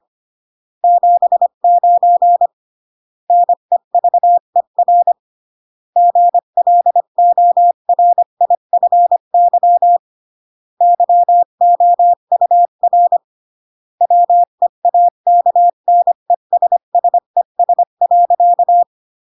Morse code, heard as a keyed tone, about 25 wpm, 700 Hz. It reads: 79 NEVER GLORIFY YOUR WEAKNESSES.